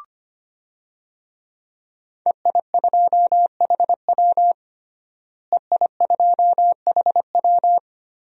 Morse code, heard as a keyed tone, about 25 wpm, 700 Hz. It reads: EI2HW EI2HW
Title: DXCC-100 Call Signs - 5 Characters - Repeated 3x - 25wpm